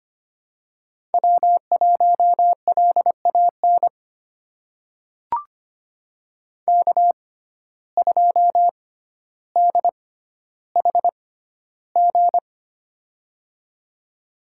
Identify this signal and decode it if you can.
Morse code, heard as a keyed tone, about 25 wpm, spaced as for 13 wpm, 700 Hz. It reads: W1LAN K2DHG
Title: USA Call Signs - 5 Characters - 6x Character Spacing - 25wpm